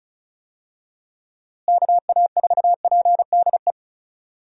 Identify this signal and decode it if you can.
Morse code, heard as a keyed tone, about 36 wpm, 700 Hz. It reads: KA4PDE
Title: USA Call Signs - 6 Characters - Repeated 2x - 35wpm